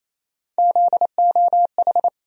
Morse code, heard as a keyed tone, about 28 wpm, 700 Hz. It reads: ZOH
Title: USA Call Sign Postfixes - 3 Letters 28wpm